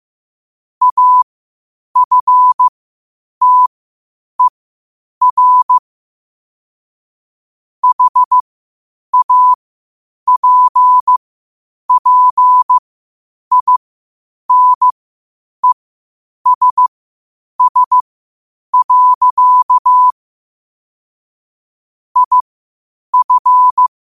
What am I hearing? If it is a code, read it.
Morse code, heard as a keyed tone, about 15 wpm, spaced as for 8 wpm, 1000 Hz. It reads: AFTER HAPPINESS. IF
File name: qotdfiles / 8wpm